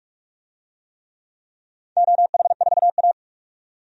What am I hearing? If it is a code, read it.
Morse code, heard as a keyed tone, about 46 wpm, 700 Hz. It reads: OH4A
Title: DXCC-100 Call Signs - 4 Characters - 45wpm